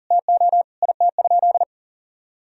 Morse code, heard as a keyed tone, about 40 wpm, 700 Hz. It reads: TO IT?
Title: Sentences from Top 1000 Words - Review Part 4 - 40wpm